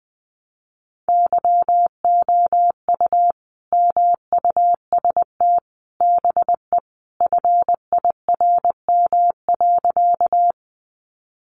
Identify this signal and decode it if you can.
Morse code, heard as a keyed tone, about 20 wpm, 700 Hz. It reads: YOU MUST BE FIRM.
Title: Sentences from Top 1400 Words 20wpm